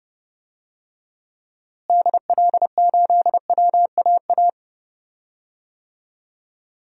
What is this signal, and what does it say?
Morse code, heard as a keyed tone, about 31 wpm, 700 Hz. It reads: DL8WAA
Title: DXCC-100 Call Signs - 6 Characters - Repeated 2x - 30wpm